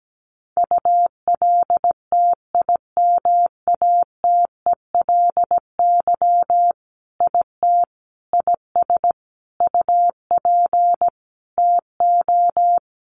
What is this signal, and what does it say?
Morse code, heard as a keyed tone, about 17 wpm, 700 Hz. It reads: ULTIMATELY IT IS UP TO
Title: Sentences from Top 1200 Words 17wpm